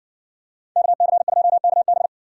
Morse code, heard as a keyed tone, about 60 wpm, 700 Hz. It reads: KC2KB